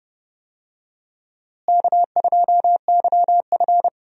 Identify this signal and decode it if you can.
Morse code, heard as a keyed tone, about 30 wpm, 700 Hz. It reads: K2YF
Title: USA Call Signs - 4 Characters - Repeated 2x - 30wpm